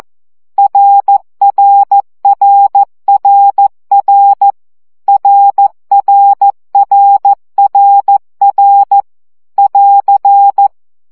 Morse code, heard as a keyed tone, about 14 wpm, 800 Hz. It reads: RRRRR RRRRR <AR>
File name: CW gyakorlas r 12 baud